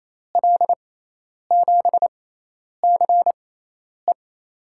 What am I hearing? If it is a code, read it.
Morse code, heard as a keyed tone, about 27 wpm, spaced as for 10 wpm, 700 Hz. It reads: L7CE
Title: USA Call Signs - 6x Character Spacing - 28wpm